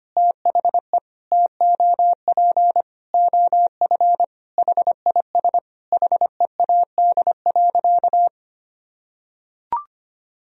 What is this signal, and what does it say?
Morse code, heard as a keyed tone, about 25 wpm, 700 Hz. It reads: THE TOP OF HIS HEAD.
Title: Sentences from Top 1000 Words - Review Part 2 - 25wpm